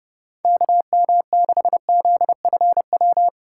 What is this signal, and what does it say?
Morse code, heard as a keyed tone, about 30 wpm, 700 Hz. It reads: KM6ZFW
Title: USA Call Signs - 6 Characters - 2x Character Spacing - 30wpm